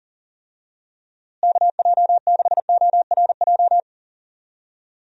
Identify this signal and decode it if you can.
Morse code, heard as a keyed tone, about 40 wpm, 700 Hz. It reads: KJ6ORJ
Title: USA Call Signs - 6 Characters - Repeated 2x - 40wpm